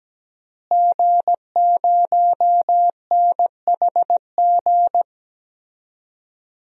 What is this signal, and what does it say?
Morse code, heard as a keyed tone, about 17 wpm, 700 Hz. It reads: G0NHG
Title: DXCC-100 Call Signs - 5 Characters - Repeated 2x - 17wpm